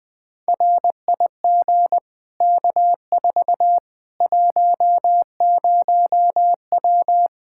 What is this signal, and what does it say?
Morse code, heard as a keyed tone, about 20 wpm, 700 Hz. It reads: RIG K4 10W
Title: Practice QSO - Exchange 2 - Info - 20wpm